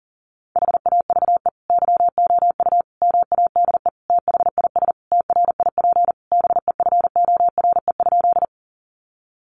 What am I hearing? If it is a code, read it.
Morse code, heard as a keyed tone, about 40 wpm, 700 Hz. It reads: HAVE YOU MADE THIS TRIP BEFORE?
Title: Sentences from Top 1300 Words 40wpm